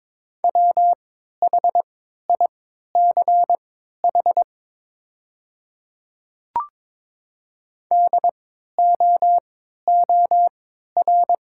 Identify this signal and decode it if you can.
Morse code, heard as a keyed tone, about 22 wpm, spaced as for 12 wpm, 690 Hz. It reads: WHICH DOOR
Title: 3x Character Spacing - Top 300 Words - 22wpm